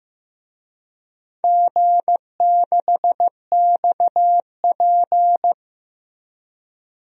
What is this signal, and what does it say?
Morse code, heard as a keyed tone, about 15 wpm, 700 Hz. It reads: G6XP
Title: DXCC-100 Call Signs - 4 Characters - Repeated 3x - 15wpm